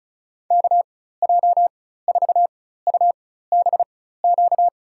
Morse code, heard as a keyed tone, about 36 wpm, spaced as for 16 wpm, 700 Hz. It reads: KJ4UBQ
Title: USA Call Signs - 6 Characters - 4x Character Spacing - 35wpm